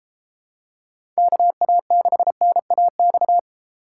Morse code, heard as a keyed tone, about 32 wpm, 700 Hz. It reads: KA6NAX